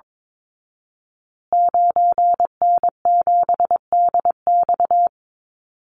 Morse code, heard as a keyed tone, about 22 wpm, 690 Hz. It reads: 9N7DX
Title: Call Signs Repeated 2x - 22wpm